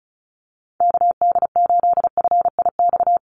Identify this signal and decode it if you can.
Morse code, heard as a keyed tone, about 36 wpm, 700 Hz. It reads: KD8FIX
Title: USA Call Signs 35wpm